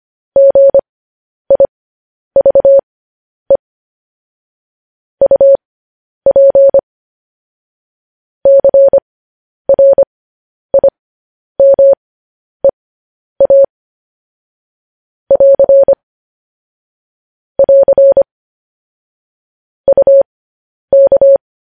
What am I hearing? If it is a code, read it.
Morse code, heard as a keyed tone, about 25 wpm, spaced as for 10 wpm, 550 Hz. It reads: GIVE UP CRIMEA <AR> <AR> UK
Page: news headlines in morse code at 10 wpm